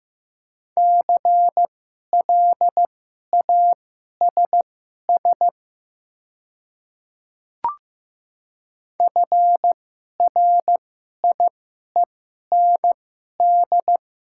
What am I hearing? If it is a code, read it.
Morse code, heard as a keyed tone, about 15 wpm, spaced as for 11 wpm, 690 Hz. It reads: CLASS FRIEND
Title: 2x Character Spacing - Top 500 Words - 15wpm